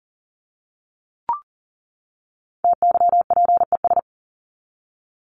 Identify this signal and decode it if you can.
Morse code, heard as a keyed tone, about 40 wpm, 700 Hz. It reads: TYPES
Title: ES:S Suffix - Top 100 Words with Base Words 40wpm